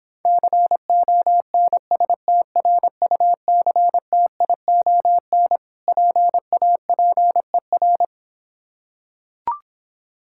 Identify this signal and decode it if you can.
Morse code, heard as a keyed tone, about 26 wpm, 690 Hz. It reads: CONSTRUCTION PAPER